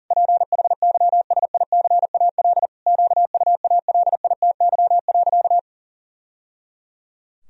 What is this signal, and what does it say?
Morse code, heard as a keyed tone, about 40 wpm, 700 Hz. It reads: PHYSICAL QUALITY.
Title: Sentences from Top 1000 Words - Review Part 4 - 40wpm